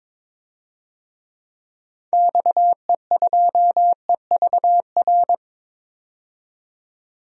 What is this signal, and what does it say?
Morse code, heard as a keyed tone, about 22 wpm, 700 Hz. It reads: XE2EVR
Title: DXCC-100 Call Signs - 6 Characters - 22wpm